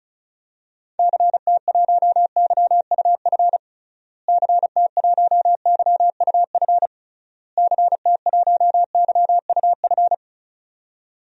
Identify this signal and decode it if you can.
Morse code, heard as a keyed tone, about 36 wpm, 700 Hz. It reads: CT1YUF CT1YUF CT1YUF
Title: DXCC-100 Call Signs - 6 Characters - Repeated 3x - 35wpm